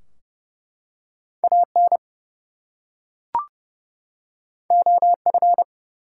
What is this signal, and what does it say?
Morse code, heard as a keyed tone, about 30 wpm, 700 Hz. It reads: AN OF
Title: Two Letter Words - 30wpm